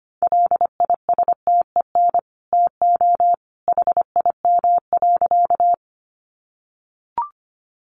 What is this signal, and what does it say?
Morse code, heard as a keyed tone, about 25 wpm, 700 Hz. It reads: LISTEN TO HIM.
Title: Sentences from Top 400 Words 25wpm